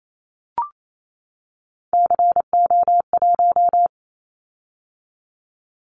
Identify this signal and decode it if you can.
Morse code, heard as a keyed tone, about 28 wpm, 700 Hz. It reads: CO1
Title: Callsign Prefixes for Top 100 DXCC - 28wpm